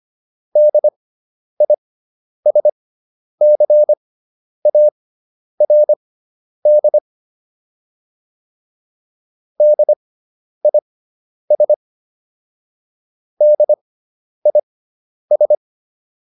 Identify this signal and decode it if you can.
Morse code, heard as a keyed tone, about 25 wpm, spaced as for 10 wpm, 600 Hz. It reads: DISCARD DIS DIS